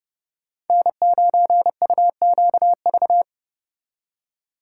Morse code, heard as a keyed tone, about 38 wpm, 700 Hz. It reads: N9UQV